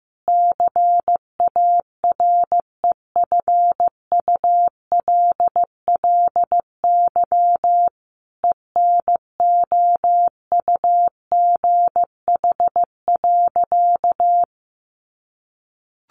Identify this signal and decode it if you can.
Morse code, heard as a keyed tone, about 15 wpm, 700 Hz. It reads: CAREFULLY ENOUGH.
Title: Sentences from Top 1300 Words 15wpm